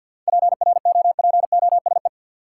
Morse code, heard as a keyed tone, about 51 wpm, 700 Hz. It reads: PROPOSE
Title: Seven Letter Words - 50wpm